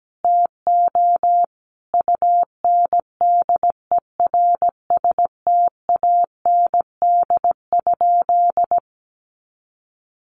Morse code, heard as a keyed tone, about 17 wpm, 700 Hz. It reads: TO UNDERSTAND?
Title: Sentences from Top 1100 Words 17wpm